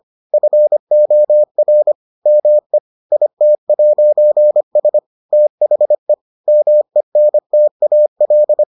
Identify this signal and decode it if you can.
Morse code, heard as a keyed tone, about 25 wpm, 600 Hz. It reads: FOR ME IT'S THE MENTAL